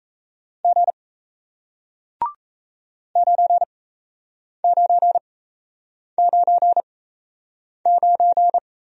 Morse code, heard as a keyed tone, about 38 wpm, spaced as for 8 wpm, 700 Hz. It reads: G 9999